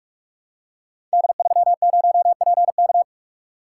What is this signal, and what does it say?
Morse code, heard as a keyed tone, about 44 wpm, 700 Hz. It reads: D30PK
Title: Code Groups - 5 Letter:Numbers - 45wpm